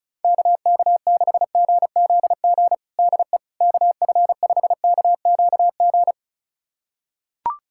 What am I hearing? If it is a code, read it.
Morse code, heard as a keyed tone, about 34 wpm, 700 Hz. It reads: KK6GZG DE KF5KQG
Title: USA Callsign DE Callsign - 6 Characters - 35wpm